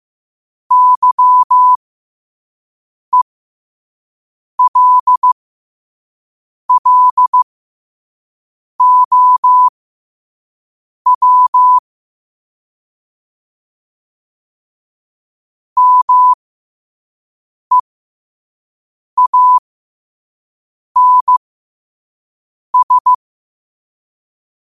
Morse code, heard as a keyed tone, about 15 wpm, spaced as for 5 wpm, 1000 Hz. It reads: YELLOW MEANS
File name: qotdfiles / 5wpm